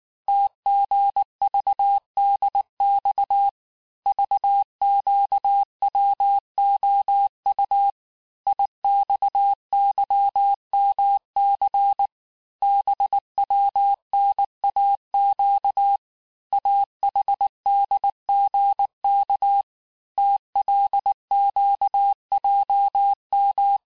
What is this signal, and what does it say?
Morse code, heard as a keyed tone, about 19 wpm, 800 Hz. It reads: TGVDX VQWOU IXYMC BWNAQ AHDGK TLQJM
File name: cours-du-lundi files / cours-1583--texte-chiffre-18-mpm